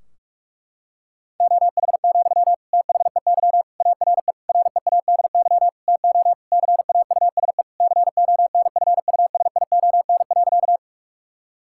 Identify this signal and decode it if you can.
Morse code, heard as a keyed tone, about 46 wpm, 700 Hz. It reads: OH, THEY ARE READY TO CAUSE CONFUSION.
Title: Sentences from Top 1600 Words 45wpm